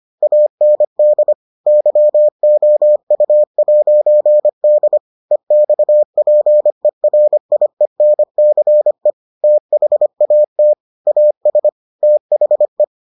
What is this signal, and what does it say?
Morse code, heard as a keyed tone, about 25 wpm, 600 Hz. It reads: AND YOU'D EXPERIENCE THAT AS THE